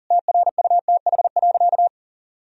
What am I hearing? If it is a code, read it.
Morse code, heard as a keyed tone, about 37 wpm, 700 Hz. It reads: TRUTH.
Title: Sentences from Top 1000 Words - Review Part 1 - 40wpm